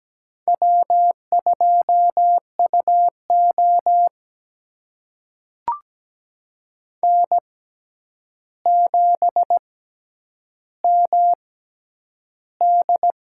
Morse code, heard as a keyed tone, about 17 wpm, spaced as for 9 wpm, 700 Hz. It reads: W2UO N7MD
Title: USA Call Signs - 4 Characters - 6x Character Spacing - 17wpm